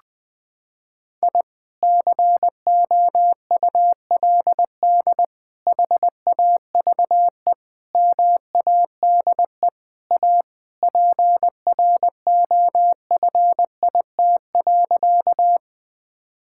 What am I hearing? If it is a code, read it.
Morse code, heard as a keyed tone, about 21 wpm, 700 Hz. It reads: I COULD HAVE MADE A PROFIT.
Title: Sentences from Top 1000 Words - Review Part 5 - 20wpm